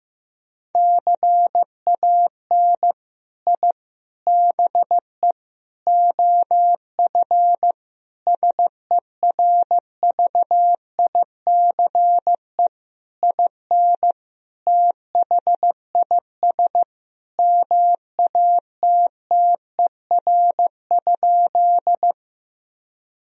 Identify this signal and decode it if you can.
Morse code, heard as a keyed tone, about 15 wpm, 700 Hz. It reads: CAN I BE OF SERVICE IN THIS MATTER?
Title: Sentences from Top 1000 Words - Review Part 4 - 15wpm